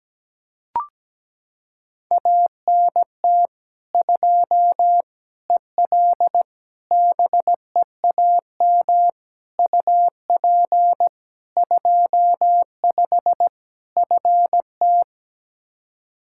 Morse code, heard as a keyed tone, about 17 wpm, 700 Hz. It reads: ANT 2 EL BEAM UP 25 FT
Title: Practice QSO - Exchange 2 - Info - 17wpm